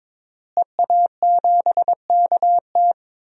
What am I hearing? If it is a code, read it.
Morse code, heard as a keyed tone, about 21 wpm, 700 Hz. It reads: EA7KT
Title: DXCC-100 Call Signs - 5 Characters - Repeated 3x - 22wpm